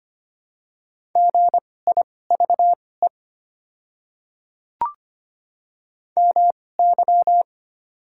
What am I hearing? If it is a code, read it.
Morse code, heard as a keyed tone, about 25 wpm, spaced as for 18 wpm, 700 Hz. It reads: GIVE MY
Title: 2x Character Spacing - Top 200 Words - 25wpm